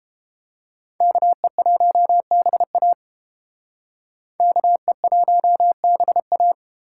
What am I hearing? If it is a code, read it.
Morse code, heard as a keyed tone, about 29 wpm, 700 Hz. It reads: KE1BA KE1BA